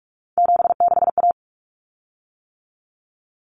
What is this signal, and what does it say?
Morse code, heard as a keyed tone, about 44 wpm, 700 Hz. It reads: 76A
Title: Common Cut Numbers - 3-Digit 45wpm